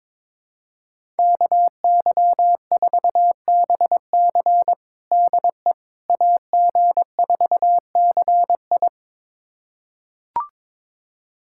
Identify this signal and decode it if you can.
Morse code, heard as a keyed tone, about 21 wpm, 700 Hz. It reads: KY4BC DE AG4CI
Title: USA Callsign DE Callsign - 5 Characters - 22wpm